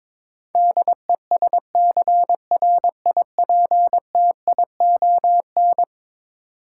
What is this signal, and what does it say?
Morse code, heard as a keyed tone, about 21 wpm, 700 Hz. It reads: DESCRIPTION
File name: practice / 22wpm / Eleven Letter Words - 22wpm